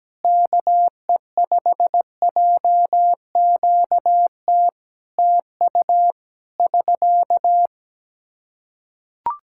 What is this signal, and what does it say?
Morse code, heard as a keyed tone, about 17 wpm, 700 Hz. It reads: KE5JQT TU <SK>
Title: Practice QSO - Exchange 4 - Ending - 17wpm